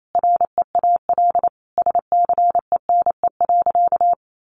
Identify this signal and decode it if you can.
Morse code, heard as a keyed tone, about 28 wpm, 700 Hz. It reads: REAL SCENE.
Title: Sentences from Top 1100 Words 28wpm